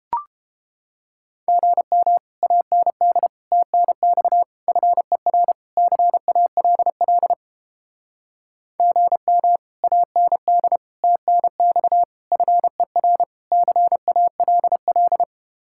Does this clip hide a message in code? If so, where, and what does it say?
Yes: Morse code from 1.5 to 15.2 s, heard as a keyed tone, about 30 wpm, 700 Hz. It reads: GM AND TNX FER CALL GM AND TNX FER CALL